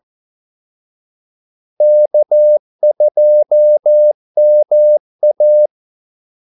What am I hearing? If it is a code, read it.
Morse code, heard as a keyed tone, about 14 wpm, 600 Hz. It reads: K2MA
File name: Callsigns-14wpm